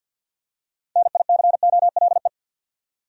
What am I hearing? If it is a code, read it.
Morse code, heard as a keyed tone, about 49 wpm, 700 Hz. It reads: NICOLE
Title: USA Common Names - 50wpm